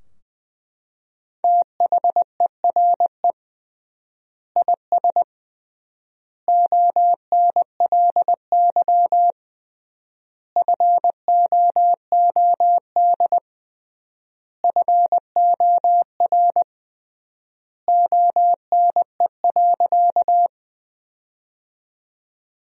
Morse code, heard as a keyed tone, about 20 wpm, 700 Hz. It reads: THERE IS ONLY FOOD FOR ONE.
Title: 3x Word-Spacing - Sentences from Top 300 Words - 20wpm